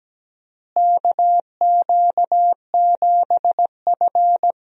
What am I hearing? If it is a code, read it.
Morse code, heard as a keyed tone, about 17 wpm, 700 Hz. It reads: KQ7F
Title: USA Call Signs - 4 Characters - Repeated 2x - 17wpm